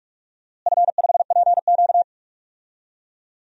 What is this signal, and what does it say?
Morse code, heard as a keyed tone, about 46 wpm, 700 Hz. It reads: F5PQ
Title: DXCC-100 Call Signs - 4 Characters - Repeated 2x - 45wpm